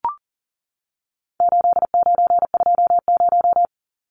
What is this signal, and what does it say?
Morse code, heard as a keyed tone, about 40 wpm, 700 Hz. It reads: 8920
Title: Numbers - 4-Digit 40wpm